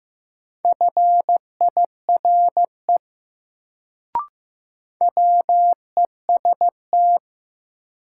Morse code, heard as a keyed tone, about 15 wpm, 700 Hz. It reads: FIRE WEST
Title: Sentences from Top 400 Words - New Word Drill - 15wpm